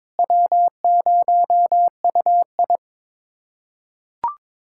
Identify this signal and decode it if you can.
Morse code, heard as a keyed tone, about 22 wpm, 700 Hz. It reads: W0UI
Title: USA Callsign DE Callsign - 4 Characters - 22wpm